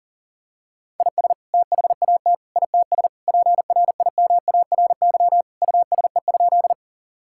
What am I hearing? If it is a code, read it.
Morse code, heard as a keyed tone, about 40 wpm, 700 Hz. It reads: IS THAT ITS PRIMARY USE?
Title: Sentences from Top 1000 Words - Review Part 5 - 40wpm